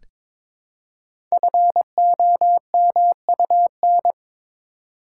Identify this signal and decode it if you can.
Morse code, heard as a keyed tone, about 21 wpm, 700 Hz. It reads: FOMUN
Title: Code Groups - 5 Letters - 22wpm